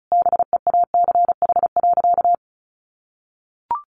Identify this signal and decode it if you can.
Morse code, heard as a keyed tone, about 34 wpm, 700 Hz. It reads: BEACH.
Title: Sentences from Top 1500 Words 35wpm